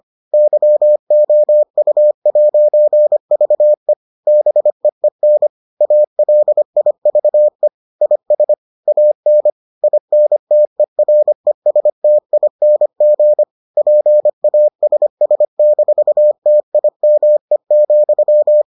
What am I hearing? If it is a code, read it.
Morse code, heard as a keyed tone, about 25 wpm, 600 Hz. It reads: YOU'VE BEEN ALIVE IS AN INTERESTING PASS-TIME,